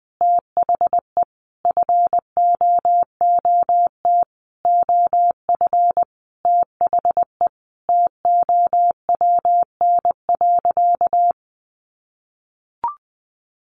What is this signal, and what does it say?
Morse code, heard as a keyed tone, about 20 wpm, 700 Hz. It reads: THE FOOT OF THE TOWN.
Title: Sentences from Top 1400 Words 20wpm